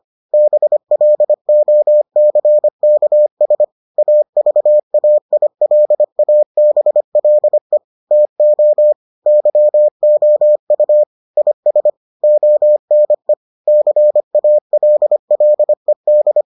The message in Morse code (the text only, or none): BLOCKS AVAILABLE TO YOU IS ONE CALLED